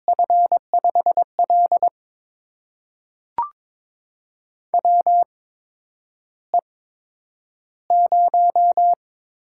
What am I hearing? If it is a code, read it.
Morse code, heard as a keyed tone, about 22 wpm, spaced as for 9 wpm, 700 Hz. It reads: F5L WE0
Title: USA Call Signs - 4 Characters - 8x Character Spacing - 22wpm